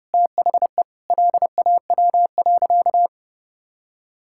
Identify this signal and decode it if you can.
Morse code, heard as a keyed tone, about 30 wpm, 700 Hz. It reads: THE LAW.
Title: Sentences from Top 1000 Words - Review Part 6 - 30wpm